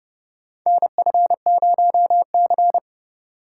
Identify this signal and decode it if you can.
Morse code, heard as a keyed tone, about 30 wpm, 700 Hz. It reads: NF0C
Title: USA Call Signs - 4 Characters - 3x Character Spacing - 30wpm